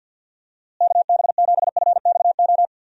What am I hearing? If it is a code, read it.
Morse code, heard as a keyed tone, about 49 wpm, 700 Hz. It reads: KB7FXO
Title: USA Call Signs - 6 Characters - Repeated 2x - 50wpm